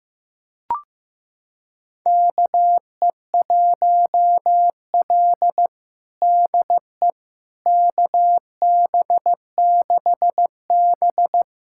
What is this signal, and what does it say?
Morse code, heard as a keyed tone, about 16 wpm, 700 Hz. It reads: KE1L DE KB6B
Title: USA Callsign DE Callsign - 4 Characters - 15wpm